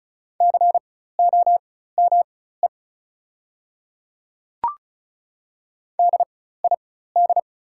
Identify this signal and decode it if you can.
Morse code, heard as a keyed tone, about 36 wpm, spaced as for 16 wpm, 700 Hz. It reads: COME DID
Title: 4x Character Spacing - Top 100 Words - 35wpm